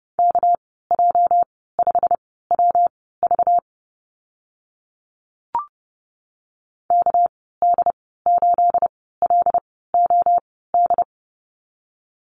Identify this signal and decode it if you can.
Morse code, heard as a keyed tone, about 30 wpm, spaced as for 17 wpm, 700 Hz. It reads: KJ5WV KD8LOD
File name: Lesson 37g - Letter x - Call Signs 30wpm